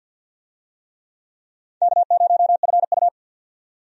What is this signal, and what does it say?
Morse code, heard as a keyed tone, about 52 wpm, 700 Hz. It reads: K0FU
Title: USA Call Signs - 4 Characters - 50wpm